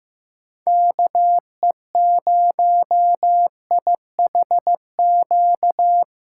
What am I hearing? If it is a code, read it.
Morse code, heard as a keyed tone, about 18 wpm, 700 Hz. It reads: KE0IHQ
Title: Speed Racing - USA 2x3 Call Signs 15wpm